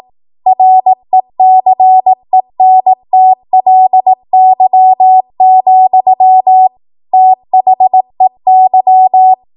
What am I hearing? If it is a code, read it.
Morse code, heard as a keyed tone, about 18 wpm, 750 Hz. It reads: RECENTLY, THEY
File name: Morse / 18WPM